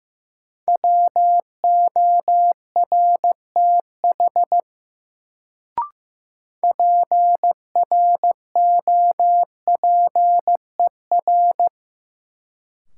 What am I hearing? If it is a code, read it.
Morse code, heard as a keyed tone, about 15 wpm, 700 Hz. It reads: WORTH PROPER
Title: Sentences from Top 800 Words - New Word Drill - 15wpm